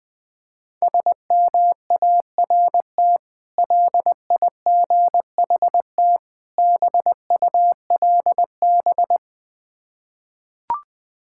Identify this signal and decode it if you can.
Morse code, heard as a keyed tone, about 21 wpm, 700 Hz. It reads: SMART LIGHT BULB
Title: Difficult Words with Hints - 20wpm